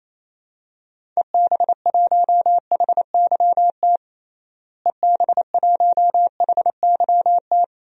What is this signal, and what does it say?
Morse code, heard as a keyed tone, about 28 wpm, 700 Hz. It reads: EB1HYT EB1HYT
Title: DXCC-100 Call Signs - 6 Characters - Repeated 3x - 28wpm